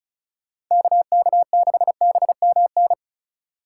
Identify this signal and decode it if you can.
Morse code, heard as a keyed tone, about 36 wpm, 700 Hz. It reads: KK6BMN
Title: Speed Racing - USA 2x3 Call Signs 30wpm